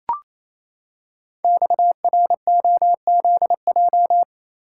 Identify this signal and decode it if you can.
Morse code, heard as a keyed tone, about 29 wpm, 700 Hz. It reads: XROZJ
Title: Code Groups - 5 Letters - 28wpm